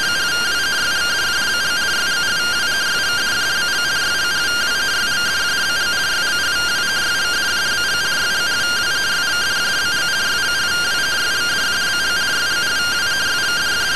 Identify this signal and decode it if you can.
Two-tone teleprinter signal, about 170 Hz shift.
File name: findings / MR AXIAL 71mm 3mm.gen-11 rtty rx